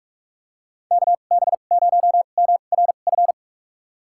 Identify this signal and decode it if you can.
Morse code, heard as a keyed tone, about 46 wpm, spaced as for 32 wpm, 700 Hz. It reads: KB0MRF
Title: USA Call Signs - 2x Character Spacing - 45wpm